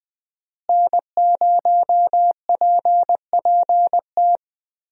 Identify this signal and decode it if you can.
Morse code, heard as a keyed tone, about 20 wpm, 700 Hz. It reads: N0PPT
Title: USA Call Signs - 5 Characters - Repeated 3x - 20wpm